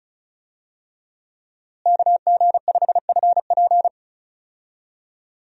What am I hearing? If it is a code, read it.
Morse code, heard as a keyed tone, about 36 wpm, 700 Hz. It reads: KG5FP
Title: USA Call Signs - 5 Characters - Repeated 3x - 35wpm